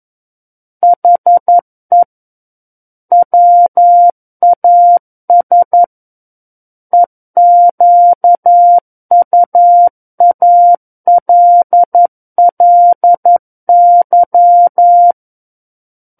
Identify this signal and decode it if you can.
Morse code, heard as a keyed tone, about 11 wpm, 700 Hz. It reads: HE WAS EQUALLY
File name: Book One / 02-Chapter 2 - 11 WPM